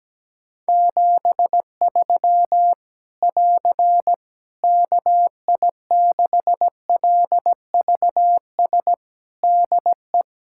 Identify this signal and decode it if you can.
Morse code, heard as a keyed tone, about 17 wpm, 700 Hz. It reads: 73 <AR> KI6LVS DE
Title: K1USN SST QSO Practice 17wpm